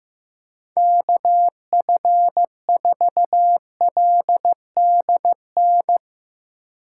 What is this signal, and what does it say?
Morse code, heard as a keyed tone, about 16 wpm, 700 Hz. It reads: KF4LDN
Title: Speed Racing - USA 2x3 Call Signs 15wpm